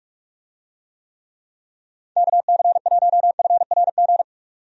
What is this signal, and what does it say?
Morse code, heard as a keyed tone, about 42 wpm, 700 Hz. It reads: KC1FRG
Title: USA Call Signs - 6 Characters - 45wpm